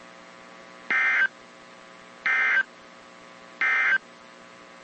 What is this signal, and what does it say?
Emergency Alert System (SAME) tones, heard as harsh buzzing bursts. It is heard from 0.8 to 4.1 s.
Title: WZ2523 - NOAA Weather Radio